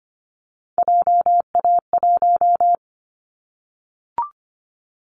Morse code, heard as a keyed tone, about 25 wpm, 700 Hz. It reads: JA1
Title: Callsign Prefixes for Top 50 DXCC - 25wpm